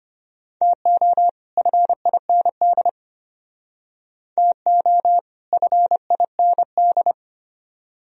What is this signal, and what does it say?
Morse code, heard as a keyed tone, about 32 wpm, 700 Hz. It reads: TO FIND TO FIND